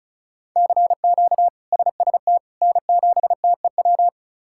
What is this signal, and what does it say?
Morse code, heard as a keyed tone, about 34 wpm, 700 Hz. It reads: CQ SST N7TEW
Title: K1USN SST QSO Practice - 35wpm